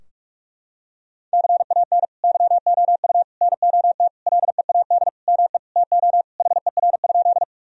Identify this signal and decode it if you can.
Morse code, heard as a keyed tone, about 42 wpm, 700 Hz. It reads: CAN YOU NOT LEAD ME TO HER?
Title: Sentences from Top 1000 Words - Review Part 2 - 45wpm